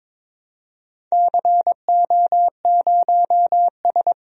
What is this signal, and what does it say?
Morse code, heard as a keyed tone, about 22 wpm, 700 Hz. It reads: CO0S
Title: DXCC-100 Call Signs - 4 Characters - 22wpm